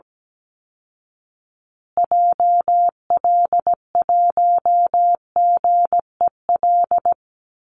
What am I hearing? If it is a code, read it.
Morse code, heard as a keyed tone, about 17 wpm, 700 Hz. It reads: JL1GEL
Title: Call Signs Repeated 3x - 17wpm